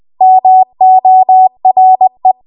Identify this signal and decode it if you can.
Morse code, heard as a keyed tone, about 20 wpm, 750 Hz. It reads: MORE
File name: Archive / 20 WPM / 20WPM